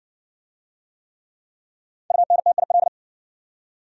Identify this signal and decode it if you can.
Morse code, heard as a keyed tone, about 60 wpm, 700 Hz. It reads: UNTIL